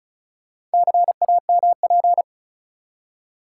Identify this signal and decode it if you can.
Morse code, heard as a keyed tone, about 36 wpm, 700 Hz. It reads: CAMP
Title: Four Letter Words - 35wpm